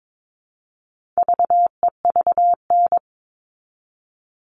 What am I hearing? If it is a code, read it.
Morse code, heard as a keyed tone, about 22 wpm, 700 Hz. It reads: VEVN
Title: All Cut Numbers - 4-Digit 22wpm